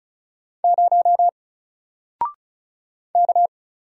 Morse code, heard as a keyed tone, about 34 wpm, 700 Hz. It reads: 0 K
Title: Single Character (Letter:Number:Punctuation:Prosign) 35wpm